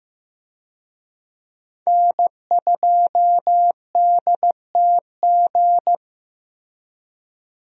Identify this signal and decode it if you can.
Morse code, heard as a keyed tone, about 16 wpm, 700 Hz. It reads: N2DTG
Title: USA Call Signs - 5 Characters - Repeated 3x - 15wpm